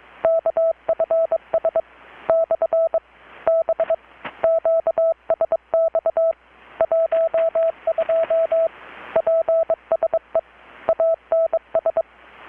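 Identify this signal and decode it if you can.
Morse code, heard as a keyed tone, about 23 wpm, 660 Hz, with radio noise. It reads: KFS / B QSX 12 PSE ANS
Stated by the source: Morse from KPH's Night of Nights XX maritime radiotelegraphy event, received in Chiapas, Mexico about 3,800 km away on a Kenwood R-600 with a grounded 10 m long-wire antenna.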